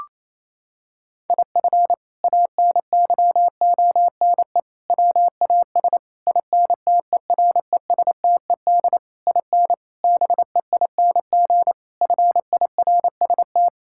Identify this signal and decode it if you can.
Morse code, heard as a keyed tone, about 28 wpm, 700 Hz. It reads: IF ANYONE WAS INTERESTED IN BEING FIRST
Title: Sentences from Top 1600 Words 28wpm